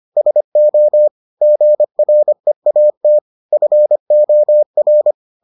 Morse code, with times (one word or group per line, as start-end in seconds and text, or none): 0.2-1.1 SO
1.4-3.2 GREAT
3.5-5.1 FOR